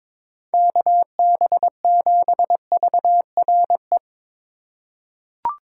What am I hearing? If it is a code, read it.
Morse code, heard as a keyed tone, about 22 wpm, 700 Hz. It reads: KB7VRE
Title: USA Call Signs - 4x Character Spacing - 22wpm